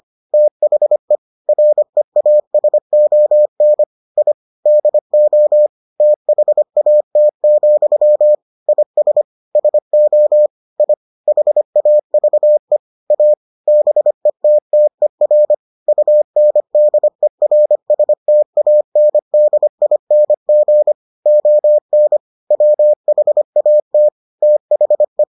Morse code, heard as a keyed tone, about 25 wpm, 600 Hz. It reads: THE REASON I DO THAT, IS SO I HAVE A BETTER UNDERSTANDING ON WHAT THE